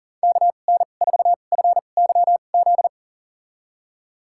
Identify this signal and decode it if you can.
Morse code, heard as a keyed tone, about 40 wpm, spaced as for 29 wpm, 690 Hz. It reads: KN4FYZ
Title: USA Call Signs - 2x Character Spacing - 40wpm